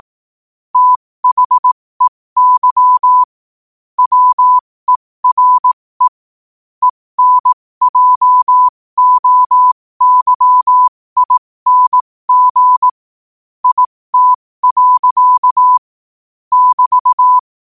Morse code, heard as a keyed tone, about 18 wpm, spaced as for 15 wpm, 1000 Hz. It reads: THEY WERE ENJOYING IT. <BT>
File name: qotdfiles / 15wpm